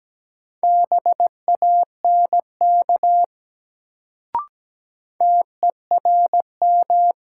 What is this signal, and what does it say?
Morse code, heard as a keyed tone, about 17 wpm, 700 Hz. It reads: BANK TERM
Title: Sentences from Top 900 Words - New Word Drill - 17wpm